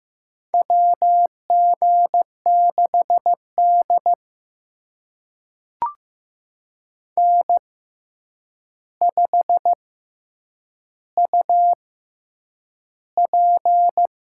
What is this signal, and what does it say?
Morse code, heard as a keyed tone, about 15 wpm, spaced as for 8 wpm, 700 Hz. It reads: WG6D N5UP
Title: USA Call Signs - 4 Characters - 6x Character Spacing - 15wpm